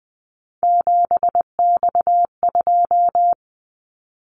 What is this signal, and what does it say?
Morse code, heard as a keyed tone, about 20 wpm, 700 Hz. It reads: 7X2
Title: Callsign Prefixes for Top 100 DXCC - 20wpm